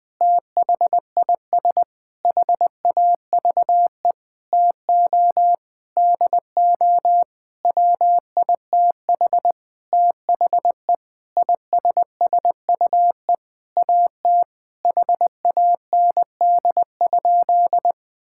Morse code, heard as a keyed tone, about 21 wpm, 700 Hz. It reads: THIS HAVE TO DO WITH THE ISSUE AT HAND?
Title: Sentences with 9-Word Phrases - 20wpm